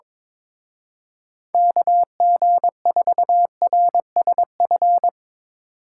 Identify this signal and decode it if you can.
Morse code, heard as a keyed tone, about 22 wpm, 700 Hz. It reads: KG4RSF
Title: Speed Racing - USA 2x3 Call Signs 22wpm